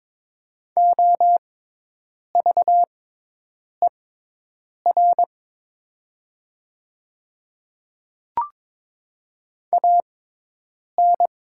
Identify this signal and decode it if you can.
Morse code, heard as a keyed tone, about 22 wpm, spaced as for 8 wpm, 700 Hz. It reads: OVER AN
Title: 6x Character Spacing - Top 100 Words - 22wpm